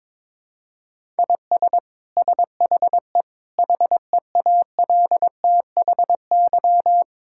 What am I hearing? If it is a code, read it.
Morse code, heard as a keyed tone, about 22 wpm, 700 Hz. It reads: IS SHE HEALTHY
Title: Sentences from Top 1000 Words - Review Part 5 - 22wpm